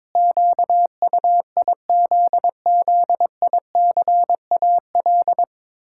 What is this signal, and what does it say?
Morse code, heard as a keyed tone, about 22 wpm, 700 Hz. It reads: QUIZZICAL
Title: Pangram Challenge v2 - 22wpm